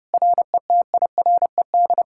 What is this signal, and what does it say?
Morse code, heard as a keyed tone, about 32 wpm, 700 Hz. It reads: RETIRED